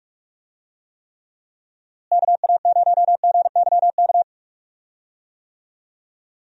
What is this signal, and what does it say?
Morse code, heard as a keyed tone, about 45 wpm, 700 Hz. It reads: KA0GYK
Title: USA Call Signs - 6 Characters - Repeated 2x - 45wpm